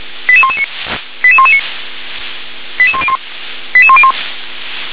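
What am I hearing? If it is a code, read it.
Five-tone selective call (ZVEI), heard as quick runs of five tones, tones of about 70 ms. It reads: 88119 88119 88101 88101
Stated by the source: Five-tone ZVEI selective call received on an SDR.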